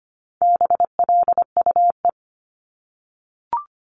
Morse code, heard as a keyed tone, about 25 wpm, 700 Hz. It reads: BLUE
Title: Multi-letter Phonemes 25wpm